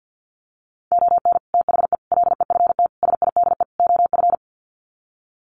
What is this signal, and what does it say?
Morse code, heard as a keyed tone, about 50 wpm, 700 Hz. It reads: ON THE LEFT SIDE OF
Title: Sets of 5 Words Encore2 50wpm